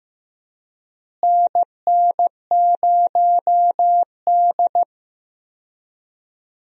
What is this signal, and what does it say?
Morse code, heard as a keyed tone, about 15 wpm, 700 Hz. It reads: NN0D
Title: USA Call Signs - 4 Characters - Repeated 2x - 15wpm